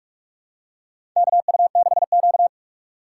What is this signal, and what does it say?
Morse code, heard as a keyed tone, about 46 wpm, 700 Hz. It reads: KU6Q
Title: USA Call Signs - 3x Character Spacing - 45wpm